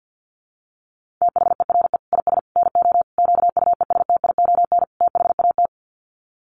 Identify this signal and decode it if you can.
Morse code, heard as a keyed tone, about 50 wpm, 700 Hz. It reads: THERE IS NO QUESTION THAT
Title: Sets of 5 Words Encore 50wpm